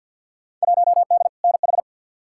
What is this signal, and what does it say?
Morse code, heard as a keyed tone, about 47 wpm, 700 Hz. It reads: 1D NH
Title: Field Day Exchanges - 50wpm